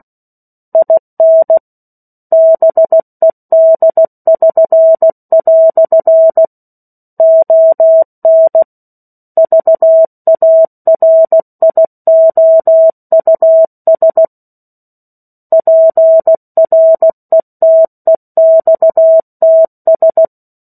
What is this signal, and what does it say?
Morse code, heard as a keyed tone, about 16 wpm, 650 Hz. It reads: IN BED<SN>" ON VARIOUS PRETEXTS
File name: 03-Chapter-03 16 WPM